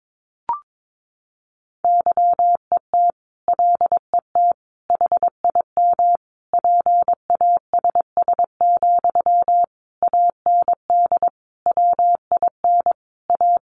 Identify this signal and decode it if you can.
Morse code, heard as a keyed tone, about 22 wpm, 700 Hz. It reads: YET LET HIM PASS, AND WIN A
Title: Sentences from Top 1500 Words 22wpm